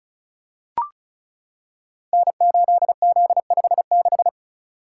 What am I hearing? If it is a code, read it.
Morse code, heard as a keyed tone, about 36 wpm, 690 Hz. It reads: N8Z56
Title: Code Groups - 5 Letter:Numbers - 35wpm